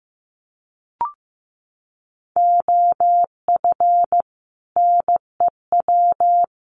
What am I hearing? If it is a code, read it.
Morse code, heard as a keyed tone, about 15 wpm, 700 Hz. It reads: OF NEW
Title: Sets of 3 Words 15wpm